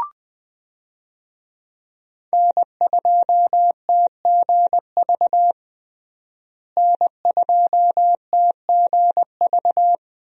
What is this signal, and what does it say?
Morse code, heard as a keyed tone, about 21 wpm, 700 Hz. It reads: N2TGV N2TGV
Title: USA Call Signs - 5 Characters - Repeated 3x - 20wpm